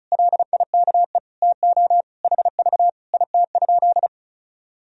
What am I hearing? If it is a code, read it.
Morse code, heard as a keyed tone, about 36 wpm, 700 Hz. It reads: LIKE TO HV IT?
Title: Sentences from Top 100 Words and Abbreviations 35wpm